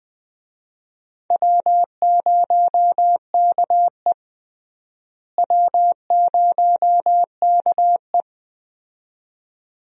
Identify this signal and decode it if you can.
Morse code, heard as a keyed tone, about 20 wpm, 700 Hz. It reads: W0KE W0KE
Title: USA Call Signs - 4 Characters - Repeated 2x - 20wpm